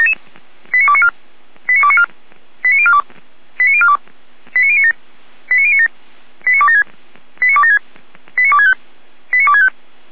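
Five-tone selective call (ZVEI), heard as quick runs of five tones, tones of about 70 ms. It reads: 89294 89294 89042 89042 89097 89097 89276 89276 89265 89265
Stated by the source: Received on an SDR, ZVEI five-tone call 89140, from the feuerwehr-orlamuende.org audio archive (Feuerwehr Orlamünde).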